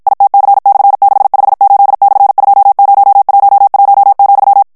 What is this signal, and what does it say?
Morse code, heard as a keyed tone, about 52 wpm, 800 Hz. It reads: ITC/658/2011,
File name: morse mp3 / ESPAGNOL TEXTE EN CLAIR 2 VITESSE 50 MOTS MINUTES